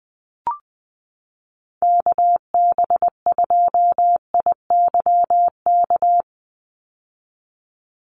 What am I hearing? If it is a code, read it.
Morse code, heard as a keyed tone, about 19 wpm, 700 Hz. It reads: KB2IYK
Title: USA Call Signs 20wpm